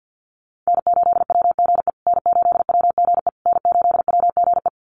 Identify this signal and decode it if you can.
Morse code, heard as a keyed tone, about 50 wpm, 700 Hz. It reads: N8WGE N8WGE N8WGE
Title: USA Call Signs Repeated 3x 50wpm